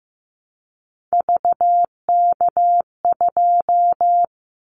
Morse code, heard as a keyed tone, about 15 wpm, 700 Hz. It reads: VK2
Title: Callsign Prefixes for Top 100 DXCC - 15wpm